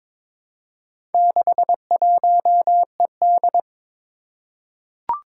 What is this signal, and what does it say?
Morse code, heard as a keyed tone, about 23 wpm, 700 Hz. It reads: 61ED
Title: Four Challenging Letter-Number Combinations 22wpm